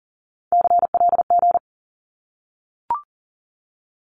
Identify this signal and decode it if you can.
Morse code, heard as a keyed tone, about 40 wpm, 700 Hz. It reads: CLG
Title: Abbreviations Repeated 3x - 40wpm